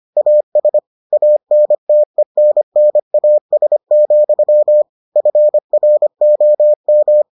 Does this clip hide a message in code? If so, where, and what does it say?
Yes: Morse code from 0.2 to 7.2 s, heard as a keyed tone, about 25 wpm, 600 Hz. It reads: AS ANTENNAS, FROM